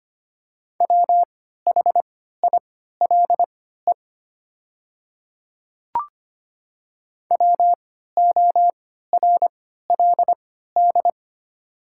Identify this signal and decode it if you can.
Morse code, heard as a keyed tone, about 25 wpm, spaced as for 14 wpm, 700 Hz. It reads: WHILE WORLD
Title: 3x Character Spacing - Top 300 Words - 25wpm